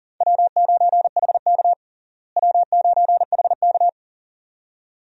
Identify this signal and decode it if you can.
Morse code, heard as a keyed tone, about 40 wpm, 700 Hz. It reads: W9HK W9HK
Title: USA Call Signs - 4 Characters - Repeated 2x - 40wpm